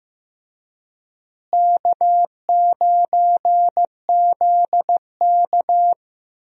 Morse code, heard as a keyed tone, about 15 wpm, 700 Hz. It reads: K9ZK
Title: USA Call Signs - 4 Characters - Repeated 2x - 15wpm